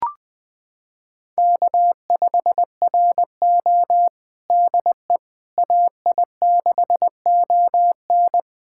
Morse code, heard as a keyed tone, about 20 wpm, 700 Hz. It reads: K5RO DE AI6ON
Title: USA Callsign DE Callsign 20wpm